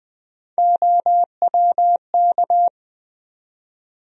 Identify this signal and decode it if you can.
Morse code, heard as a keyed tone, about 20 wpm, 700 Hz. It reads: OWK
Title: USA Call Sign Postfixes - 3 Letters 20wpm